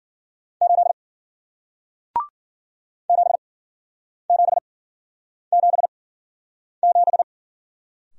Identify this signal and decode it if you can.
Morse code, heard as a keyed tone, about 54 wpm, spaced as for 9 wpm, 700 Hz. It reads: 8 7777